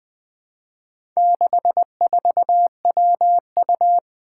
Morse code, heard as a keyed tone, about 20 wpm, 700 Hz. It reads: 64WU
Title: Four Letter-Number Combinations - 20wpm